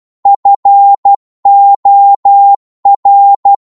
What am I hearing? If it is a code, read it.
Morse code, heard as a keyed tone, about 12 wpm, 800 Hz. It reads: FOR
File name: Morse / psg-12wpm